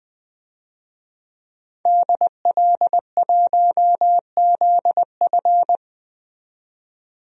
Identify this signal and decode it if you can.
Morse code, heard as a keyed tone, about 20 wpm, 700 Hz. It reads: DL1ZF
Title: DXCC-100 Call Signs - 5 Characters - 20wpm